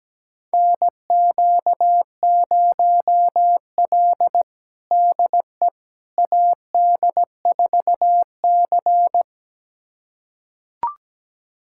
Morse code, heard as a keyed tone, about 17 wpm, 700 Hz. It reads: NQ0L DE AD4C
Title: USA Callsign DE Callsign - 4 Characters - 17wpm